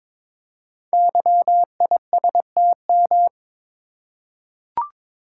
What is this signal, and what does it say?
Morse code, heard as a keyed tone, about 22 wpm, 700 Hz. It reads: YISTM
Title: Code Groups - 5 Letters - 22wpm